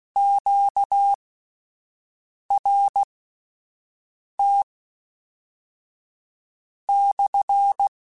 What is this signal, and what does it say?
Morse code, heard as a keyed tone, about 16 wpm, spaced as for 5 wpm, 800 Hz. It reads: QRT/
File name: Q Calls / Q Calls-mp3s / Q Calls SPSL1 @10